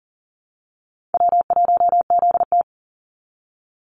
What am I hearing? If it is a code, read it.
Morse code, heard as a keyed tone, about 39 wpm, 700 Hz. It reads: W1ZT
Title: Easy Call Signs Repeated 2x - 40wpm